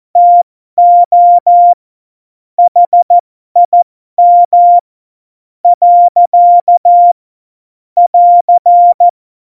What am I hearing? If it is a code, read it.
Morse code, heard as a keyed tone, about 14 wpm, spaced as for 12 wpm, 700 Hz. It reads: TO HIM . <AR>
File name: Morsekurs / Witze